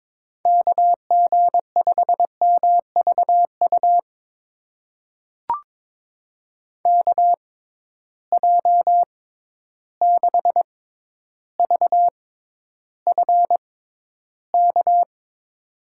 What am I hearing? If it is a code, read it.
Morse code, heard as a keyed tone, about 22 wpm, spaced as for 11 wpm, 700 Hz. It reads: KG5MVU KJ6VFK
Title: USA Call Signs - 6x Character Spacing - 22wpm